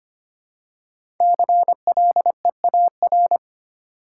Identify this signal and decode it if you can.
Morse code, heard as a keyed tone, about 25 wpm, 700 Hz. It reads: CLEAR
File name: 5 Letter Words - First 3 Letters Given - 25wpm